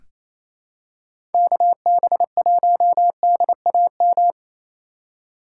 Morse code, heard as a keyed tone, about 28 wpm, 700 Hz. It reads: KB1DAM
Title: USA Call Signs - 6 Characters - 4x Character Spacing - 28wpm